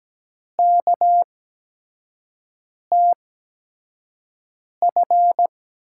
Morse code, heard as a keyed tone, about 17 wpm, spaced as for 5 wpm, 700 Hz. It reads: KTF
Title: USA Call Signs - 5 Characters - 8x Character Spacing - 17wpm